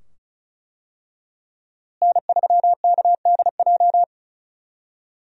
Morse code, heard as a keyed tone, about 34 wpm, 700 Hz. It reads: N3KDJ
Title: USA Call Signs - 5 Characters - Repeated 2x - 35wpm